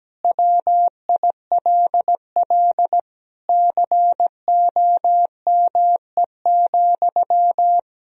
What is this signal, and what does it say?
Morse code, heard as a keyed tone, about 17 wpm, 700 Hz. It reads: WILL COME,
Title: Sentences from Top 100 Words and Abbreviations 17wpm